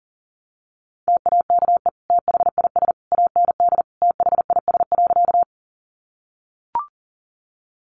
Morse code, heard as a keyed tone, about 40 wpm, 700 Hz. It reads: TAKE THIS AND THIS.
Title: Sentences from Top 200 Words 40wpm